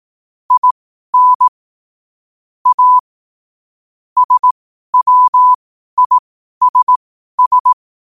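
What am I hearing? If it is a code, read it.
Morse code, heard as a keyed tone, about 18 wpm, spaced as for 12 wpm, 1000 Hz. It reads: IN A SWISS
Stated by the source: Morse code sent at 12 words per minute.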